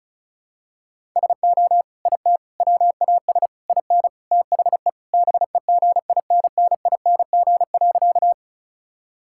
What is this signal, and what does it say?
Morse code, heard as a keyed tone, about 36 wpm, 700 Hz. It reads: SO IT WAS IN THE BEGINNING.
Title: Sentences from Top 1000 Words - Review Part 5 - 35wpm